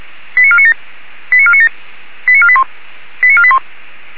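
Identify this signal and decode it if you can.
Five-tone selective call (ZVEI), heard as quick runs of five tones, tones of about 70 ms. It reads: 89497 89497 89481 89481
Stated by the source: ZVEI five-tone selective call received on SDR.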